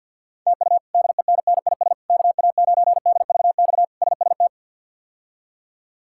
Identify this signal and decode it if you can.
Morse code, heard as a keyed tone, about 49 wpm, 700 Hz. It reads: TU DENNIS KA9DVX SST
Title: K1USN SST QSO Practice - 50wpm